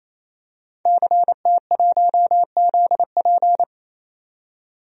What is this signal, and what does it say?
Morse code, heard as a keyed tone, about 28 wpm, 700 Hz. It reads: CT1ZP
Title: DXCC-100 Call Signs - 5 Characters - 28wpm